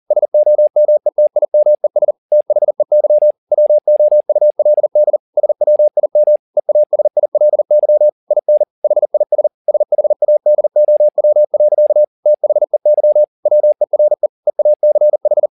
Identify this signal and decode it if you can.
Morse code, heard as a keyed tone, about 40 wpm, 600 Hz. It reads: SOMETIMES THEY WOULD SWIM EASILY IN HIS SHADOW. THEY WERE EACH
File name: the old man and the sea 2 40wpm